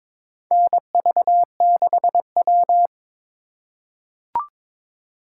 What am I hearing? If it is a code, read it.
Morse code, heard as a keyed tone, about 22 wpm, 700 Hz. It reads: NV6W
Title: USA Call Signs - 4 Characters - 4x Character Spacing - 22wpm